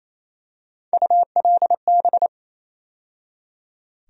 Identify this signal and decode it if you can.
Morse code, heard as a keyed tone, about 28 wpm, 700 Hz. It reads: ULB
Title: USA Call Sign Postfixes - 3 Letters 28wpm